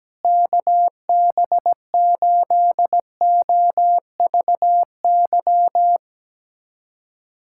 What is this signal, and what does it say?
Morse code, heard as a keyed tone, about 17 wpm, 700 Hz. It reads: KB8OVY
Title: USA Callsign DE Callsign - 6 Characters - 17wpm